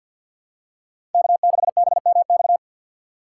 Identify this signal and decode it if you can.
Morse code, heard as a keyed tone, about 52 wpm, 700 Hz. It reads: K6BMX
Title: USA Call Signs - 5 Characters - 4x Character Spacing - 50wpm